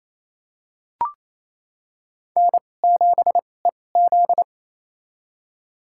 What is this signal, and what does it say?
Morse code, heard as a keyed tone, about 28 wpm, spaced as for 20 wpm, 700 Hz. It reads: N7EZ
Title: USA Call Signs - 4 Characters - 2x Character Spacing - 28wpm